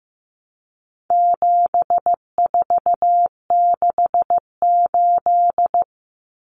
Morse code, heard as a keyed tone, about 15 wpm, 700 Hz. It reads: 7468
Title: Numbers - 4-Digit 15wpm